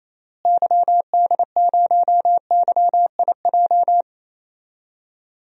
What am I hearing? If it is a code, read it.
Morse code, heard as a keyed tone, about 28 wpm, 700 Hz. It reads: YD0YIJ
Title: DXCC-100 Call Signs - 6 Characters - Repeated 3x - 28wpm